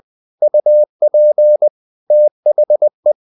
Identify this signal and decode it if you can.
Morse code, heard as a keyed tone, about 20 wpm, 600 Hz. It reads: UP THE
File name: the old man and the sea part 2 20wpm